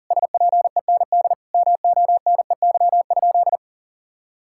Morse code, heard as a keyed tone, about 40 wpm, 700 Hz. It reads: SPEND MONEY?
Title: Sentences from Top 1000 Words - Review Part 5 - 40wpm